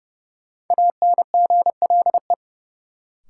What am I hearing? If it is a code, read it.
Morse code, heard as a keyed tone, about 30 wpm, 700 Hz. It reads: ANGLE
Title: Sentences from Top 1500 Words - New Word Drill Alt - 30wpm